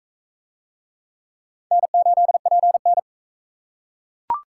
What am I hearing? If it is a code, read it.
Morse code, heard as a keyed tone, about 37 wpm, 700 Hz. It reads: N8PN